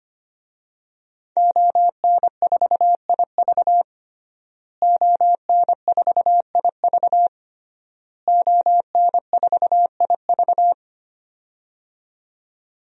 Morse code, heard as a keyed tone, about 25 wpm, 700 Hz. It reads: ON4IV ON4IV ON4IV
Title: DXCC-100 Call Signs - 5 Characters - Repeated 3x - 25wpm